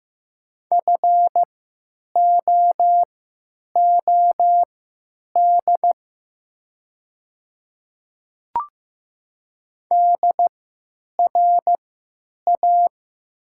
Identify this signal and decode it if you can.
Morse code, heard as a keyed tone, about 15 wpm, spaced as for 9 wpm, 700 Hz. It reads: FOOD DRA
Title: 3x Character Spacing - Top 300 Words - 15wpm